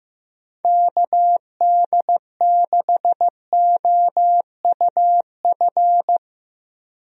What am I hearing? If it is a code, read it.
Morse code, heard as a keyed tone, about 15 wpm, 700 Hz. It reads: KD6OUF
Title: USA Call Signs - 6 Characters - Repeated 3x - 15wpm